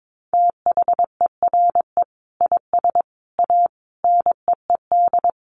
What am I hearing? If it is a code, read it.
Morse code, heard as a keyed tone, about 22 wpm, 700 Hz. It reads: THERE IS A NEED
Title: Sets of 5 Words Encore2 22wpm